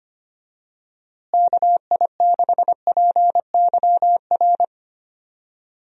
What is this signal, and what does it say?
Morse code, heard as a keyed tone, about 25 wpm, 700 Hz. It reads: KI6PYR
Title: USA Call Signs - 6 Characters - Repeated 2x - 25wpm